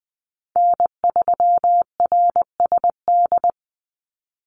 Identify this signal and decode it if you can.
Morse code, heard as a keyed tone, about 20 wpm, 700 Hz. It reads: N3RSD
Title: Call Signs Repeated 3x - 20wpm